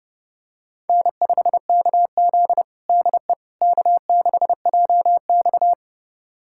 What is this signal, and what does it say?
Morse code, heard as a keyed tone, about 32 wpm, 700 Hz. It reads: N5KZ DE K6JX
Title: USA Callsign DE Callsign - 4 Characters - 30wpm